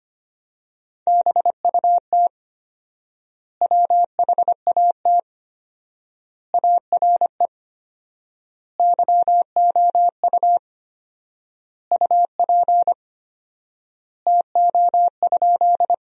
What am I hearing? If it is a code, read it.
Morse code, heard as a keyed tone, about 25 wpm, 700 Hz. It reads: BUT WHAT ARE YOU UP TO?
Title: 4x Word-Spacing - Sentences from Top 100 Words - 25wpm